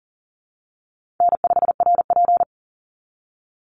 Morse code, heard as a keyed tone, about 40 wpm, 700 Hz. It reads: N5RP
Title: Easy Call Signs Repeated 3x - 40wpm